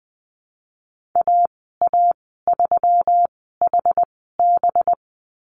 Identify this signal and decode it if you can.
Morse code, heard as a keyed tone, about 20 wpm, spaced as for 14 wpm, 700 Hz. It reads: AA3HB
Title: 193 Lesson 31g - Letter v - Call Signs 20wpm